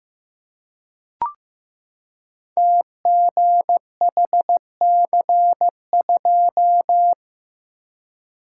Morse code, heard as a keyed tone, about 15 wpm, 700 Hz. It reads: TGHC2
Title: Code Groups - 5 Letter:Numbers - 15wpm